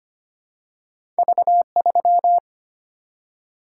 Morse code, heard as a keyed tone, about 25 wpm, 700 Hz. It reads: V3
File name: Two Letter-Number Combinations - 25wpm